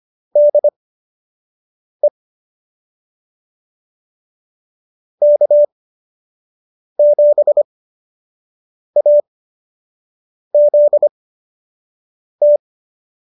Morse code, heard as a keyed tone, about 25 wpm, spaced as for 6 wpm, 590 Hz. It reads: DE K7AZT